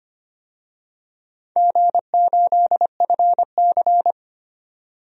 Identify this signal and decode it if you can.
Morse code, heard as a keyed tone, about 25 wpm, 700 Hz. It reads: G8FC
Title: DXCC-100 Call Signs - 4 Characters - 25wpm